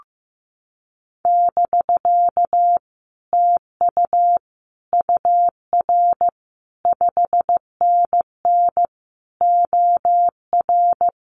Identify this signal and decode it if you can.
Morse code, heard as a keyed tone, about 15 wpm, 700 Hz. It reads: <BK> TU UR 5NN OR
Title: Common QSO Phrases Encore 15wpm